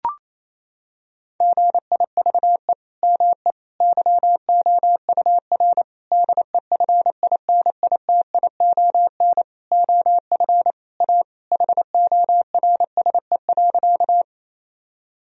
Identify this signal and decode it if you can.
Morse code, heard as a keyed tone, about 28 wpm, 700 Hz. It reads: GIVE ME YOUR DEFINITION OF A HORSE.
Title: Sentences from Top 1000 Words - Review Part 4 - 28wpm